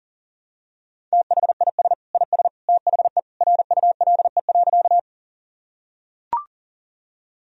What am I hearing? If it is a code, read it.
Morse code, heard as a keyed tone, about 40 wpm, 700 Hz. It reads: THIS IS THE RULE.
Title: Sentences from Top 1000 Words - Review Part 2 - 40wpm